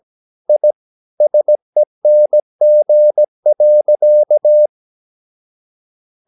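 Morse code, heard as a keyed tone, about 17 wpm, 600 Hz. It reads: I SENG.
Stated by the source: Slow (QRS) Morse practice at 17 wpm, from CW med Gnister.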